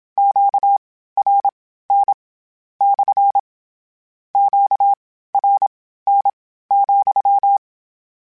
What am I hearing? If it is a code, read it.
Morse code, heard as a keyed tone, about 27 wpm, spaced as for 15 wpm, 800 Hz. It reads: QRN / QRN,
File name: Q Calls / Q Calls-mp3s / Q CALLS SPSL @25